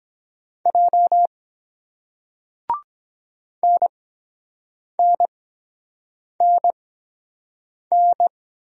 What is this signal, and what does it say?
Morse code, heard as a keyed tone, about 23 wpm, spaced as for 7 wpm, 700 Hz. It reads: J NNNN